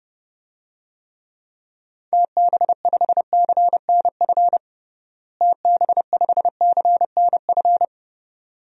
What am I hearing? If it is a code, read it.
Morse code, heard as a keyed tone, about 29 wpm, 700 Hz. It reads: TB5CNF TB5CNF
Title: DXCC-100 Call Signs - 6 Characters - Repeated 2x - 30wpm